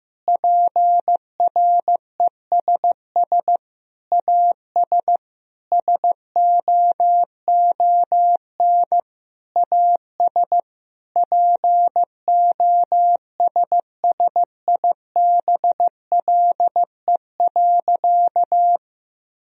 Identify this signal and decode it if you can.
Morse code, heard as a keyed tone, about 15 wpm, 700 Hz. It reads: PRESS AS SOON AS POSSIBLE.
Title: Sentences from Top 1000 Words - Review Part 1 - 15wpm